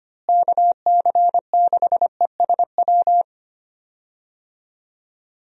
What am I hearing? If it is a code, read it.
Morse code, heard as a keyed tone, about 25 wpm, 700 Hz. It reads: KC6ESW
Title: USA Call Signs - 6 Characters - Repeated 2x - 25wpm